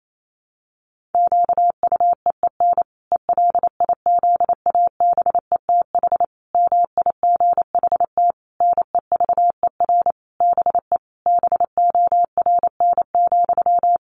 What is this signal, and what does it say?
Morse code, heard as a keyed tone, about 28 wpm, 700 Hz. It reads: QUEEN ELIZABETH MIGHT NEVER BE BORN,
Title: A Sound Of Thunder 28wpm